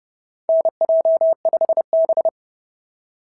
Morse code, heard as a keyed tone, about 30 wpm, 660 Hz. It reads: NJ5B
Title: SOTA QSO Practice 30wpm